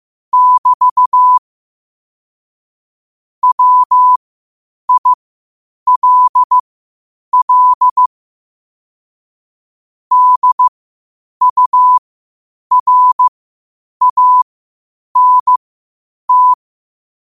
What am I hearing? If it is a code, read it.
Morse code, heard as a keyed tone, about 15 wpm, spaced as for 8 wpm, 1000 Hz. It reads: <BT> WILL DURANT
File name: qotdfiles / 8wpm